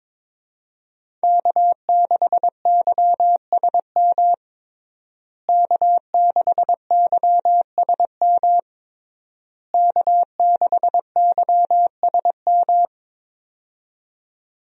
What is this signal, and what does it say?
Morse code, heard as a keyed tone, about 22 wpm, 700 Hz. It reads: K6YSM K6YSM K6YSM
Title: USA Call Signs - 5 Characters - Repeated 3x - 22wpm